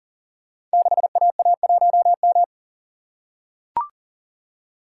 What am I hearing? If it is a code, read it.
Morse code, heard as a keyed tone, about 38 wpm, 700 Hz. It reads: 6AA1M
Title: Code Groups - 5 Letter:Numbers - 40wpm